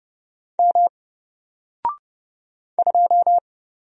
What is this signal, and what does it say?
Morse code, heard as a keyed tone, about 31 wpm, 700 Hz. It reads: M 2
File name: Single-Letter-Number-30wpm